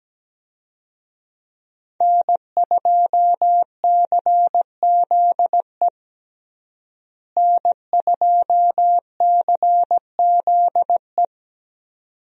Morse code, heard as a keyed tone, about 17 wpm, 700 Hz. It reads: N2CZE N2CZE
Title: USA Call Signs - 5 Characters - Repeated 2x - 17wpm